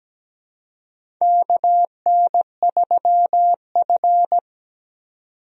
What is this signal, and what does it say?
Morse code, heard as a keyed tone, about 17 wpm, 700 Hz. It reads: KN3F
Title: USA Call Signs - 4 Characters - 17wpm